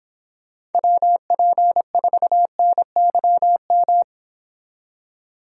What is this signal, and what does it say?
Morse code, heard as a keyed tone, about 29 wpm, 700 Hz. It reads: WP4NYM